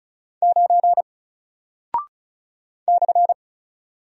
Morse code, heard as a keyed tone, about 36 wpm, 700 Hz. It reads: 9 /
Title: Single Character (Letter:Number:Punctuation:Prosign) Alt 35wpm